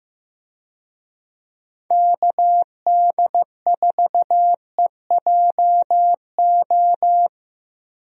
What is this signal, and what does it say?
Morse code, heard as a keyed tone, about 15 wpm, 700 Hz. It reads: KD4EJO
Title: USA Call Signs - 6 Characters - Repeated 3x - 15wpm